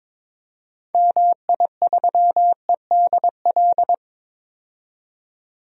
Morse code, heard as a keyed tone, about 22 wpm, 700 Hz. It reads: MI3EDL
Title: DXCC-100 Call Signs - 6 Characters - 22wpm